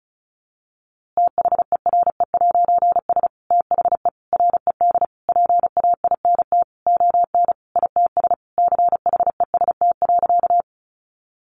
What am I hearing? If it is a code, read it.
Morse code, heard as a keyed tone, about 34 wpm, 700 Hz. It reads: THERE'S THE RED PAINT ON ITS CHEST.
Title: A Sound Of Thunder 35wpm